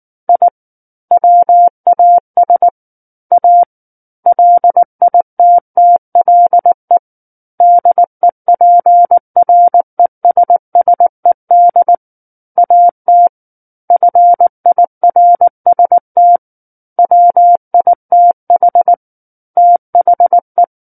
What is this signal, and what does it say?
Morse code, heard as a keyed tone, about 19 wpm, 700 Hz. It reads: I WAS A LITTLE DEPRESSED AT FIRST WITH THE
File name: Book One / 10-Chapter 10 - 19 WPM